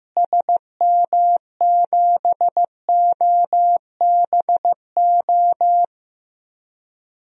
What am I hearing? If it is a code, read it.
Morse code, heard as a keyed tone, about 15 wpm, 700 Hz. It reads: SM7OBO
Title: DXCC-100 Call Signs - 6 Characters - Repeated 3x - 15wpm